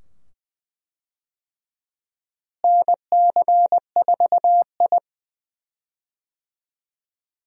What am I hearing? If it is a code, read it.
Morse code, heard as a keyed tone, about 20 wpm, 700 Hz. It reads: NC4I
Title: USA Call Signs - 4 Characters - Repeated 2x - 20wpm